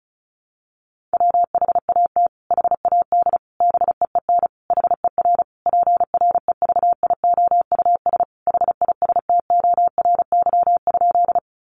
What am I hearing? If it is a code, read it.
Morse code, heard as a keyed tone, about 34 wpm, 700 Hz. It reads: WHAT HAD BEEN HER PREVIOUS HISTORY?
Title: Sentences from Top 1500 Words 35wpm